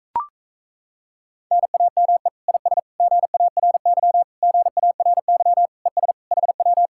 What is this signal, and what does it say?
Morse code, heard as a keyed tone, about 38 wpm, 700 Hz. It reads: NAME IS GARY GARY ES HW